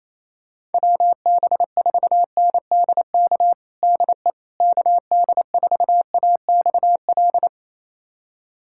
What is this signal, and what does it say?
Morse code, heard as a keyed tone, about 28 wpm, 700 Hz. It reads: WB4NDK DE KD4AXL
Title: USA Callsign DE Callsign - 6 Characters - 28wpm